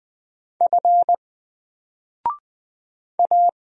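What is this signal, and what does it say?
Morse code, heard as a keyed tone, about 20 wpm, 700 Hz. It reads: F A
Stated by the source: Single letter and number Morse drill at 20 wpm, from N4LCD.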